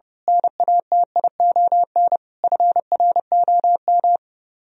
Morse code, heard as a keyed tone, about 28 wpm, 700 Hz. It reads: NATION FROM
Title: Speed-Racing - Sets of 5 Words 30wpm